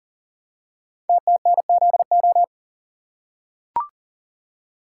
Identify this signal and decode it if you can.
Morse code, heard as a keyed tone, about 38 wpm, 690 Hz. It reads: TTNZO
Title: Code Groups - 5 Letters - 40wpm